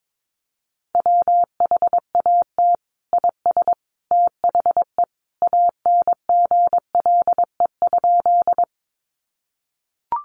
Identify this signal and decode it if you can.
Morse code, heard as a keyed tone, about 22 wpm, 700 Hz. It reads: WHAT IS THE ANGLE?
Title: Sentences from Top 1500 Words 22wpm